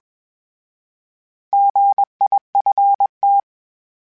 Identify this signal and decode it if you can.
Morse code, heard as a keyed tone, about 21 wpm, 800 Hz. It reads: GIFT
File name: Misc / Misc-mp3s / Misc @20